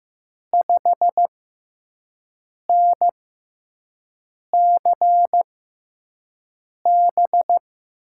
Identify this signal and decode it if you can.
Morse code, heard as a keyed tone, about 15 wpm, spaced as for 5 wpm, 700 Hz. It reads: 5NCB
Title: USA Call Signs - 6 Characters - 6x Character Spacing - 15wpm